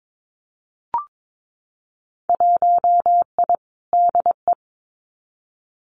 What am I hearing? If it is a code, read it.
Morse code, heard as a keyed tone, about 22 wpm, 700 Hz. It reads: 1I DE
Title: Winter Field Day Exchanges Encore 22wpm